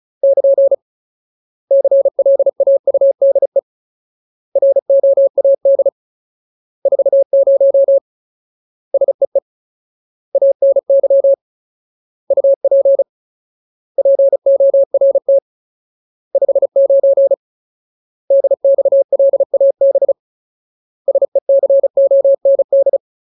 Morse code, heard as a keyed tone, about 34 wpm, 550 Hz. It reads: <KN> CLAUDE ROAD 40 SEE ANY UP PORT 59 DXLAB SECOND
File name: Ultimate Word List / 35WPM Ultimate